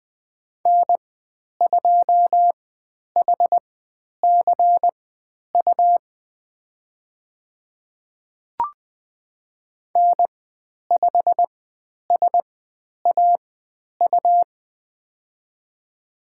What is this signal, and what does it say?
Morse code, heard as a keyed tone, about 20 wpm, spaced as for 10 wpm, 700 Hz. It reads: N2HCU N5SAU
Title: 109 Lesson 18g - Letter m - Call Signs 20wpm